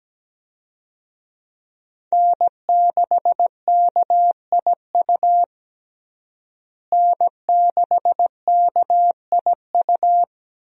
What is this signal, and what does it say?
Morse code, heard as a keyed tone, about 17 wpm, 700 Hz. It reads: N6KIU N6KIU
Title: USA Call Signs - 5 Characters - Repeated 3x - 17wpm